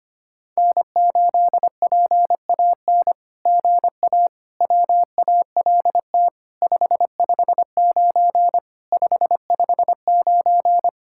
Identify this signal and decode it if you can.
Morse code, heard as a keyed tone, about 24 wpm, 700 Hz. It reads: N8PAN GA WALT 559 559
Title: SOTA QSO Practice 25wpm